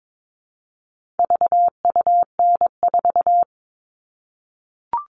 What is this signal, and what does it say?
Morse code, heard as a keyed tone, about 22 wpm, 700 Hz. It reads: VUN4
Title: All Cut Numbers - 4-Digit 22wpm